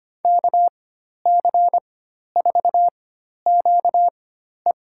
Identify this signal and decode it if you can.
Morse code, heard as a keyed tone, about 25 wpm, spaced as for 12 wpm, 700 Hz. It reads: KC4QE
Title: USA Call Signs - 5 Characters - 4x Character Spacing - 25wpm